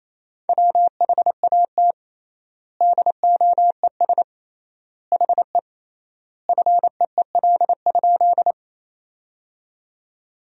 Morse code, heard as a keyed tone, about 28 wpm, 700 Hz. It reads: WHAT DOES HE FEEL?
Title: 3x Word-Spacing - Sentences from Top 400 Words - 28wpm